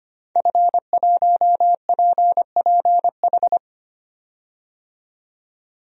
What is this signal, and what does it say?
Morse code, heard as a keyed tone, about 25 wpm, 700 Hz. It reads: F1PPH
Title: DXCC-100 Call Signs - 5 Characters - 25wpm